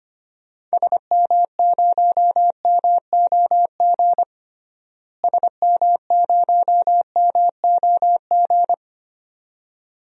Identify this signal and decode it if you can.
Morse code, heard as a keyed tone, about 25 wpm, 700 Hz. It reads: SM0MOG SM0MOG
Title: DXCC-100 Call Signs - 6 Characters - Repeated 2x - 25wpm